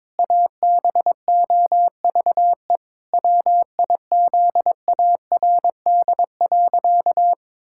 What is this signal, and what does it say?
Morse code, heard as a keyed tone, about 21 wpm, 700 Hz. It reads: ABOVE WIZARD.
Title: Pangram Challenge v2 - 22wpm